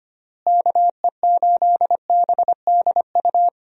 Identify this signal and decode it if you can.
Morse code, heard as a keyed tone, about 25 wpm, 700 Hz. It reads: KE8BDU
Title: USA Call Signs - 6 Characters - Repeated 3x - 25wpm